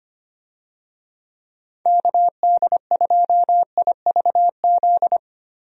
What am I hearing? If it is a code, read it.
Morse code, heard as a keyed tone, about 25 wpm, 700 Hz. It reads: KD2IVZ
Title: USA Call Signs - 6 Characters - Repeated 2x - 25wpm